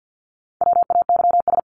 Morse code, heard as a keyed tone, about 48 wpm, 700 Hz. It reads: WAYS
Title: ES:S Suffix - Top 100 Words with Base Words 50wpm